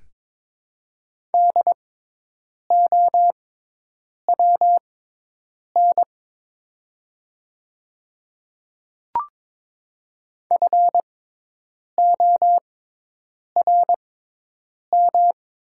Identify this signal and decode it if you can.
Morse code, heard as a keyed tone, about 22 wpm, spaced as for 8 wpm, 700 Hz. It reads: DOWN FORM
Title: 6x Character Spacing - Top 200 Words - 22wpm